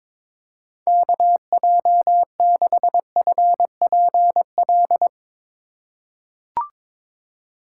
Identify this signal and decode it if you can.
Morse code, heard as a keyed tone, about 22 wpm, 700 Hz. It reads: KJ6FPL
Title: USA Call Signs - 6 Characters - 4x Character Spacing - 22wpm